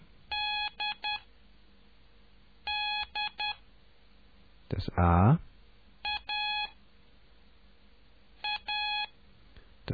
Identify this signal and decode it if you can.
Morse code, heard as a keyed tone, about 10 wpm, spaced as for 4 wpm, 2500 Hz. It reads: DD AA